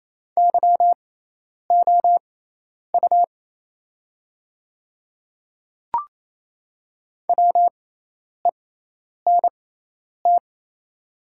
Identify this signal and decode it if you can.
Morse code, heard as a keyed tone, about 28 wpm, spaced as for 10 wpm, 700 Hz. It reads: YOU WENT
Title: 6x Character Spacing - Top 200 Words - 28wpm